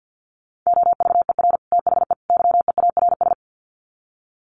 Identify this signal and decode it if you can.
Morse code, heard as a keyed tone, about 50 wpm, 700 Hz. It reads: OVER THE YEARS
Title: Sets of 3 Words 50wpm